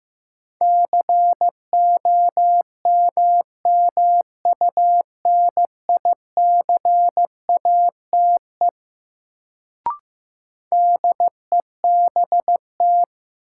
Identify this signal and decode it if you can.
Morse code, heard as a keyed tone, about 16 wpm, 700 Hz. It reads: COMMUNICATE DEBT
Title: Sentences from Top 1300 Words - New Word Drill - 15wpm